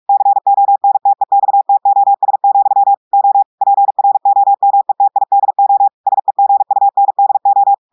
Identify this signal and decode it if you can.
Morse code, heard as a keyed tone, about 46 wpm, 800 Hz. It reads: CONTEXTOS, O PROMETIDO SEGUNDO
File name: portugais texte en clair 1 vitesse 45 mots minutes